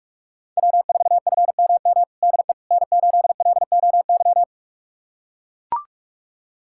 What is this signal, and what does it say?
Morse code, heard as a keyed tone, about 43 wpm, 700 Hz. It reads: W4FMM DE N8LOY
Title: USA Callsign DE Callsign - 5 Characters - 45wpm